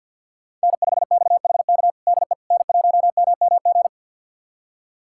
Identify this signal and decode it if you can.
Morse code, heard as a keyed tone, about 49 wpm, 700 Hz. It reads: N5XHK DE N1MMG
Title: USA Callsign DE Callsign - 5 Characters - 50wpm